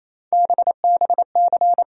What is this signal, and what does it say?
Morse code, heard as a keyed tone, about 28 wpm, 700 Hz. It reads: BBC
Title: USA Call Sign Postfixes - 3 Letters 28wpm